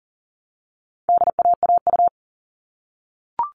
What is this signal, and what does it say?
Morse code, heard as a keyed tone, about 40 wpm, 700 Hz. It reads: DAAU
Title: All Cut Numbers - 4-Digit 40wpm